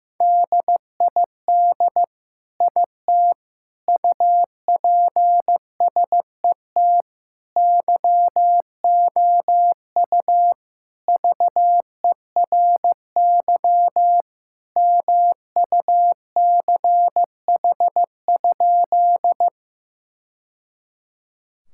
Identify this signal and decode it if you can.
Morse code, heard as a keyed tone, about 15 wpm, 700 Hz. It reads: DID IT UPSET YOU VERY MUCH?
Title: Sentences from Top 1000 Words - Review Part 5 - 15wpm